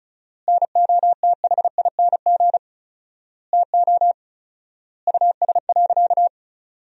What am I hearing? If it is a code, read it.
Morse code, heard as a keyed tone, about 36 wpm, 700 Hz. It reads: NOTHING TO US.
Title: 4x Word-Spacing - Sentences from Top 500 Words - 35wpm